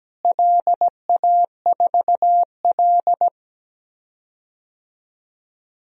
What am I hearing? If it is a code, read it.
Morse code, heard as a keyed tone, about 17 wpm, 700 Hz. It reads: LA4L
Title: DXCC-100 Call Signs - 4 Characters - Repeated 3x - 17wpm